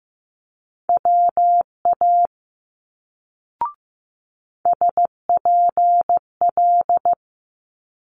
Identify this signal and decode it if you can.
Morse code, heard as a keyed tone, about 15 wpm, 700 Hz. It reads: WA SPL
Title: Multi-letter Phonemes Quick Fire 15wpm